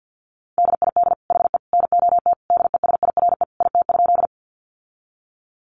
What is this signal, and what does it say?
Morse code, heard as a keyed tone, about 48 wpm, 700 Hz. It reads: DID HE NOT DESIRE IT?
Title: Sentences from Top 1100 Words 50wpm